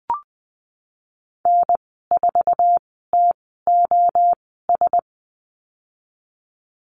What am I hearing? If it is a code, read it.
Morse code, heard as a keyed tone, about 19 wpm, spaced as for 14 wpm, 700 Hz. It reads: N4TOS
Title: 088 Lesson 15g - Number 5 - Call Signs 20wpm